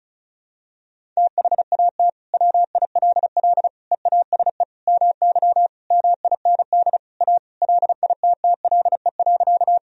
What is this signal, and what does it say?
Morse code, heard as a keyed tone, about 36 wpm, 700 Hz. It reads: THAT WILL EASE MY MIND A LITTLE.
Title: Sentences from Top 1000 Words - Review Part 1 - 35wpm